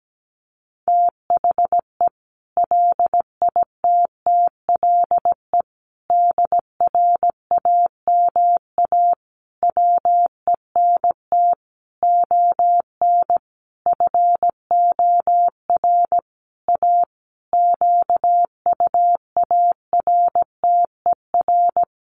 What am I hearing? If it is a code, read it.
Morse code, heard as a keyed tone, about 17 wpm, 700 Hz. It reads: THE LITTLE DRAMA WENT ON FOR A QUARTER
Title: Sentences from Top 1500 Words 17wpm